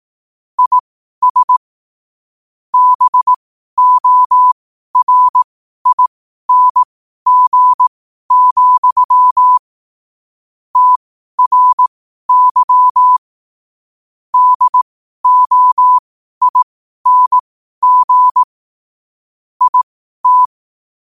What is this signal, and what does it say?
Morse code, heard as a keyed tone, about 18 wpm, spaced as for 12 wpm, 1000 Hz. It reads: IS BORING, TRY DOING IT
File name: qotdfiles / 12wpm